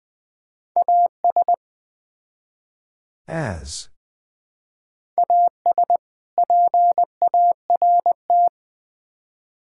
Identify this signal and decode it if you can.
Morse code, heard as a keyed tone, about 21 wpm, 700 Hz. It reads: AS AS PART
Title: Build A Phrase - 3-Word Phrases - 20wpm